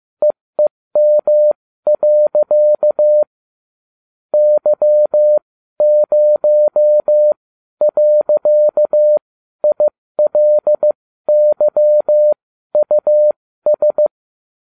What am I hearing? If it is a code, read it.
Morse code, heard as a keyed tone, about 15 wpm, spaced as for 12 wpm, 600 Hz. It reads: EEM. Y0.ILYUS